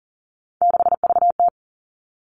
Morse code, heard as a keyed tone, about 39 wpm, 690 Hz. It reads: 6VT
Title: Three Challenging Letter-Number Combinations 40wpm